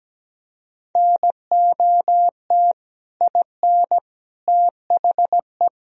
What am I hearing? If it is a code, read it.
Morse code, heard as a keyed tone, about 17 wpm, 700 Hz. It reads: NOT IN THE
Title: Sentences from Top 1000 Words - Review Part 4 - 17wpm